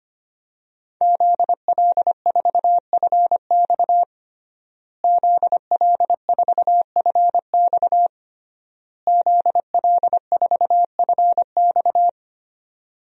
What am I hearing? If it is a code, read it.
Morse code, heard as a keyed tone, about 25 wpm, 700 Hz. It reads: ZL4FX ZL4FX ZL4FX
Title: DXCC-100 Call Signs - 5 Characters - Repeated 3x - 25wpm